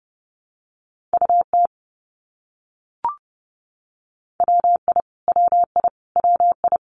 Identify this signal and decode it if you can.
Morse code, heard as a keyed tone, about 30 wpm, 700 Hz. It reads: UT WI WI WI
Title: US States Repeated 3x 30wpm